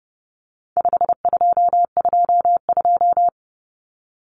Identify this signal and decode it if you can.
Morse code, heard as a keyed tone, about 29 wpm, 700 Hz. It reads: 5222
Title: Numbers - 4-Digit 30wpm